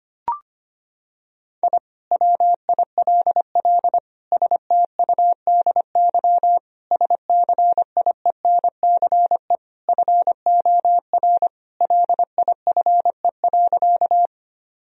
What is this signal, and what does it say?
Morse code, heard as a keyed tone, about 25 wpm, 700 Hz. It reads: I WILL STUDY SCIENCE FOR LIFE.
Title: Sentences from Top 1000 Words - Review Part 2 - 25wpm